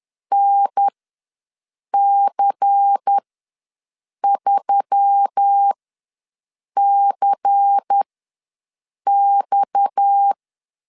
Morse code, heard as a keyed tone, about 10 wpm, spaced as for 6 wpm, 800 Hz. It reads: NC3CX
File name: LEZIONI / 6 PKX0 / 6 PKX ws 10wpm 3706